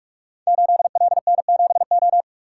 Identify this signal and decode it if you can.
Morse code, heard as a keyed tone, about 44 wpm, 700 Hz. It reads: 8LN7O
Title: Code Groups - 5 Letter:Numbers - 45wpm